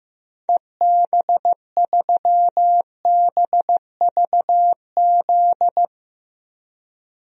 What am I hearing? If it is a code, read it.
Morse code, heard as a keyed tone, about 15 wpm, 700 Hz. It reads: EB3BVZ
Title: DXCC-100 Call Signs - 6 Characters - 15wpm